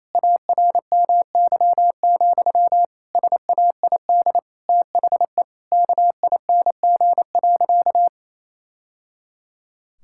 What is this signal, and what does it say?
Morse code, heard as a keyed tone, about 28 wpm, 700 Hz. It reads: ARMY, SAID THE KING.
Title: Sentences from Top 1000 Words - Review Part 5 - 28wpm